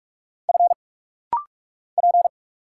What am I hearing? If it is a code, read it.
Morse code, heard as a keyed tone, about 46 wpm, 700 Hz. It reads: F P
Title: Single Letters - Rapid-Fire 45wpm